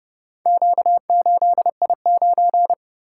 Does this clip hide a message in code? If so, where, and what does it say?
Yes: Morse code from 0.5 to 2.7 s, heard as a keyed tone, about 29 wpm, 700 Hz. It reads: Q8I9